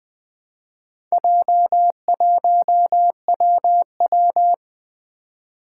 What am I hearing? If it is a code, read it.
Morse code, heard as a keyed tone, about 20 wpm, 700 Hz. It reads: J1WW
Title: Four Challenging Letter-Number Combinations 20wpm